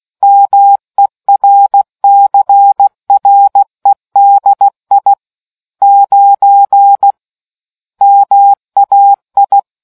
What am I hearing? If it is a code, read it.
Morse code, heard as a keyed tone, about 16 wpm, 800 Hz. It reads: MERCREDI 9 MAI